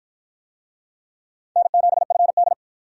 Morse code, heard as a keyed tone, about 53 wpm, 700 Hz. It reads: N7FD